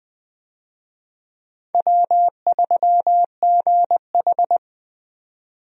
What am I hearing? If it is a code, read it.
Morse code, heard as a keyed tone, about 20 wpm, 700 Hz. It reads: W3GH
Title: USA Call Signs - 4 Characters - 20wpm